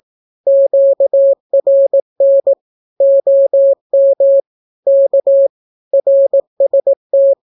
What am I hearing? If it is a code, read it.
Morse code, heard as a keyed tone, about 18 wpm, 550 Hz. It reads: QRN OM K RST
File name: naukapodstaw / skroty / skroty